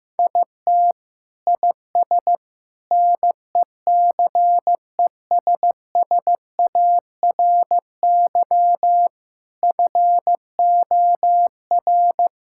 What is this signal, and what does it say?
Morse code, heard as a keyed tone, about 16 wpm, 690 Hz. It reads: IT IS NECESSARY FOR
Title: Sentences from Top 1000 Words - Review Part 5 - 15wpm